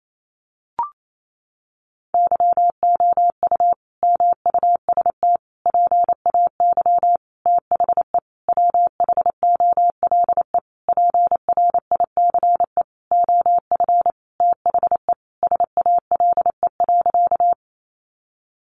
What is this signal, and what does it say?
Morse code, heard as a keyed tone, about 28 wpm, 700 Hz. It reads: YOU MUST PAY THE WHOLE PRICE OF THE SALE.
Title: Sentences from Top 1400 Words 28wpm